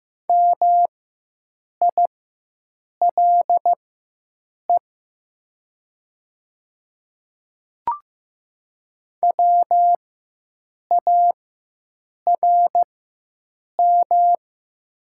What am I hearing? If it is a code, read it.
Morse code, heard as a keyed tone, about 15 wpm, spaced as for 7 wpm, 700 Hz. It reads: MILE WARM
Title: 4x Character Spacing - Top 500 Words - 15wpm